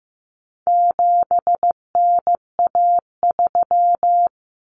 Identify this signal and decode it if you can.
Morse code, heard as a keyed tone, about 15 wpm, 700 Hz. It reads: 7NA3
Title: Common Cut Numbers - 4-Digit 15wpm